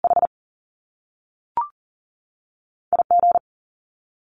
Morse code, heard as a keyed tone, about 38 wpm, 700 Hz. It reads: H IG
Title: Top Two Letter Combinations in Words 40wpm